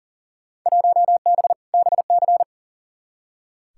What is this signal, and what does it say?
Morse code, heard as a keyed tone, about 40 wpm, 700 Hz. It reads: 1B BC
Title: Field Day Exchanges - 40wpm